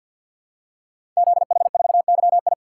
Morse code, heard as a keyed tone, about 51 wpm, 700 Hz. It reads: ZH4YI
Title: Code Groups - 5 Letter:Numbers - 50wpm